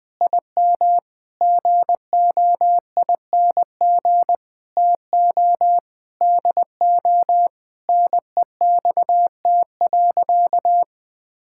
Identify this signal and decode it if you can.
Morse code, heard as a keyed tone, about 21 wpm, 700 Hz. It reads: IM GOING TO DO NEXT.
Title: Sentences with 8-Word Phrases - 20wpm